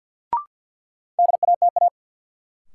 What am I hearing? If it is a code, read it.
Morse code, heard as a keyed tone, about 46 wpm, 700 Hz. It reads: DATA
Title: Sentences from Top 1000 Words - Review Part 3 - New Word Drill - 50wpm